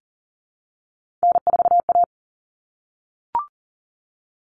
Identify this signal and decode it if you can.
Morse code, heard as a keyed tone, about 40 wpm, 700 Hz. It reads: N4A
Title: Common Cut Numbers - 3-Digit 40wpm